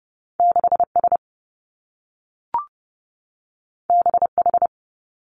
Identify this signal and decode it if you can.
Morse code, heard as a keyed tone, about 30 wpm, 700 Hz. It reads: 6 S B H
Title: Two Challenging Letter-Number Combinations 30wpm